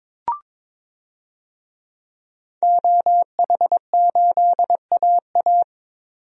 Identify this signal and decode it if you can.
Morse code, heard as a keyed tone, about 22 wpm, 700 Hz. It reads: OH8AA
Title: DXCC-100 Call Signs - 5 Characters - 22wpm